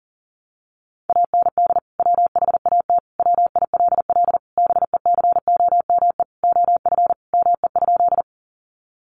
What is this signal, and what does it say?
Morse code, heard as a keyed tone, about 40 wpm, 700 Hz. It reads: AND WHAT WILL BECOME OF ME?
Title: Sentences from Top 600 Words 40wpm